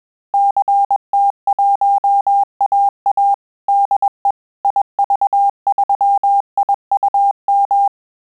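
Morse code, heard as a keyed tone, about 21 wpm, 800 Hz. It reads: CT1AA DE IV3IUM
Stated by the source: Morse QSO practice at 20 words per minute, in Italian and English.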